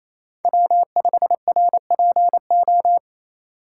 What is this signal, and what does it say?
Morse code, heard as a keyed tone, about 27 wpm, 700 Hz. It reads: W5RPO
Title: USA Call Signs - 5 Characters - 4x Character Spacing - 28wpm